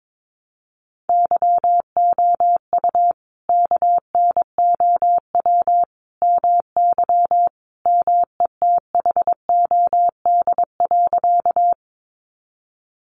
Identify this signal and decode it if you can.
Morse code, heard as a keyed tone, about 22 wpm, 700 Hz. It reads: YOU KNOW MY METHOD.
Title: Sentences from Top 600 Words 22wpm